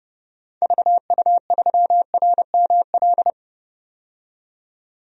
Morse code, heard as a keyed tone, about 30 wpm, 700 Hz. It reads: VU3RML
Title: DXCC-100 Call Signs - 6 Characters - 30wpm